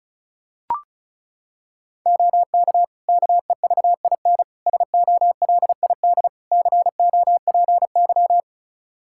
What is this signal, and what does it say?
Morse code, heard as a keyed tone, about 36 wpm, 700 Hz. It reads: OK KEVIN SOLID COPY
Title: Practice QSO - Exchange 3 - More Info - 35wpm